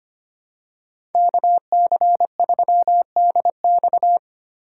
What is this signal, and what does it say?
Morse code, heard as a keyed tone, about 25 wpm, 700 Hz. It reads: KC3DX
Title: USA Call Signs - 4x Character Spacing - 25wpm